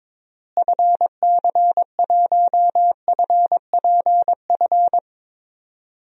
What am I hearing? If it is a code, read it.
Morse code, heard as a keyed tone, about 21 wpm, 700 Hz. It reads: FC1FPF
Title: DXCC-100 Call Signs - 6 Characters - Repeated 3x - 22wpm